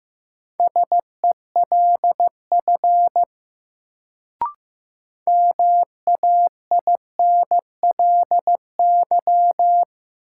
Morse code, heard as a keyed tone, about 15 wpm, 700 Hz. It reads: SELF MAINLY
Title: Sentences from Top 1400 Words - New Word Drill Alt - 15wpm